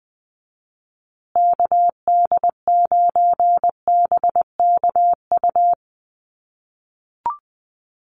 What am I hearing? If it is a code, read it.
Morse code, heard as a keyed tone, about 20 wpm, 700 Hz. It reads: KD9BKU
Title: USA Call Signs Repeated 2x 20wpm